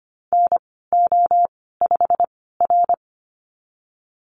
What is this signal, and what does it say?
Morse code, heard as a keyed tone, about 25 wpm, spaced as for 16 wpm, 700 Hz. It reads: NO5R
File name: Lesson 15g - Number 5 - Call Signs 25wpm